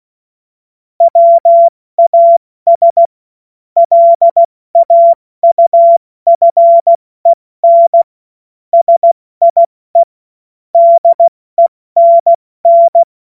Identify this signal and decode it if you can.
Morse code, heard as a keyed tone, about 16 wpm, 680 Hz. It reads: WAS LAUFEN SIE DENN
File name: Morsekurs / Witze / Witz 20. Juni 2023 70 F80